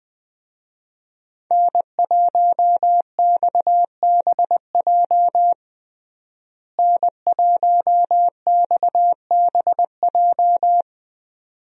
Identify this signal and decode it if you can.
Morse code, heard as a keyed tone, about 20 wpm, 700 Hz. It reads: N1XBJ N1XBJ
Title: USA Call Signs - 5 Characters - Repeated 2x - 20wpm